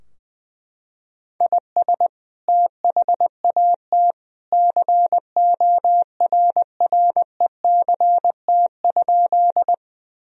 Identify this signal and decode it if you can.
Morse code, heard as a keyed tone, about 20 wpm, 700 Hz. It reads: IS THAT CORRECT?
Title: Sentences from Top 1000 Words - Review Part 2 - 20wpm